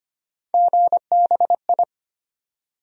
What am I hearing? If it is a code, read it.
Morse code, heard as a keyed tone, about 25 wpm, 700 Hz. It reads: GBI
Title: USA Call Sign Postfixes - 3 Letters 25wpm